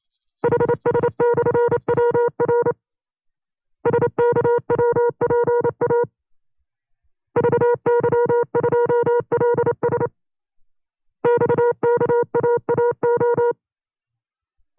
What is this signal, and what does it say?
Morse code, heard as a keyed tone, about 28 wpm, 480 Hz. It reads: HS/WR SKWPA VY2LS XKAAO
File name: FreeFiles / CW-practice / 25WPM-side2